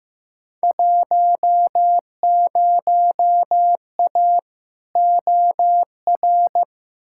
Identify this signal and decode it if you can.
Morse code, heard as a keyed tone, about 15 wpm, 700 Hz. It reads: 10A OR
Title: Field Day Exchanges - 15wpm